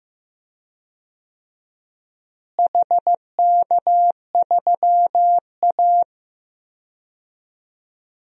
Morse code, heard as a keyed tone, about 15 wpm, 700 Hz. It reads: HK3A
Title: DXCC-100 Call Signs - 4 Characters - Repeated 3x - 15wpm